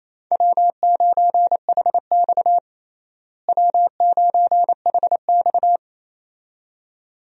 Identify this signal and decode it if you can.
Morse code, heard as a keyed tone, about 28 wpm, 700 Hz. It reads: W9HX W9HX
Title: USA Call Signs - 4 Characters - Repeated 2x - 28wpm